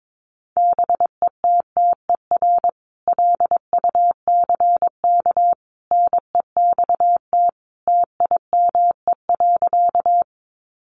Morse code, heard as a keyed tone, about 22 wpm, 700 Hz. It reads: BETTER LUCK NEXT TIME.
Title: Sentences from Top 1500 Words 22wpm